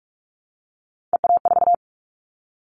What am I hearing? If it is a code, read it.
Morse code, heard as a keyed tone, about 46 wpm, 700 Hz. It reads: EA4
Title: Callsign Prefixes for Top 25 DXCC - 45wpm